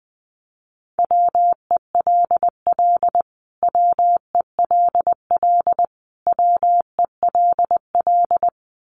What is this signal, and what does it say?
Morse code, heard as a keyed tone, about 20 wpm, 700 Hz. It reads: WELL WELL WELL
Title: Top 500 Words Repeated 20wpm